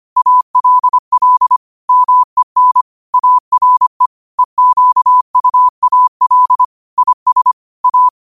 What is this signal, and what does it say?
Morse code, heard as a keyed tone, about 25 wpm, 1000 Hz. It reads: ALL MEN ARE EQUAL IS A